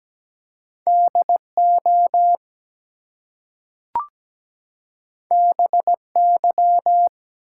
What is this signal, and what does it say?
Morse code, heard as a keyed tone, about 17 wpm, 700 Hz. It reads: DO BY
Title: Two Letter Words - 17wpm